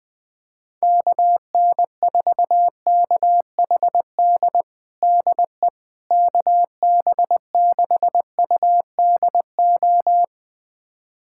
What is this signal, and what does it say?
Morse code, heard as a keyed tone, about 20 wpm, 700 Hz. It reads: KN4KHD DE KB6UDO
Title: USA Callsign DE Callsign 20wpm